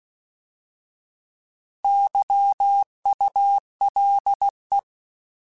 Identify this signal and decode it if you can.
Morse code, heard as a keyed tone, about 16 wpm, 800 Hz. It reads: YULE
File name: Alphabetical / Alphabetical mp3s / Ys @15